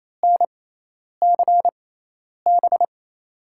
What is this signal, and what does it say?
Morse code, heard as a keyed tone, about 28 wpm, spaced as for 10 wpm, 700 Hz. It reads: NCB
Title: USA Call Signs - 6 Characters - 6x Character Spacing - 28wpm